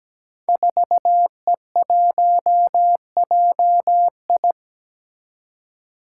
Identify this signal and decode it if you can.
Morse code, heard as a keyed tone, about 17 wpm, 700 Hz. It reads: 4E1JI
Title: Five Challenging Letter-Number Combinations 17wpm